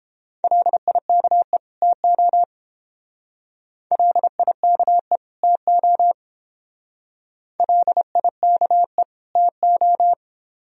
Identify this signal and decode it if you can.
Morse code, heard as a keyed tone, about 28 wpm, 700 Hz. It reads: LIKE TO LIKE TO LIKE TO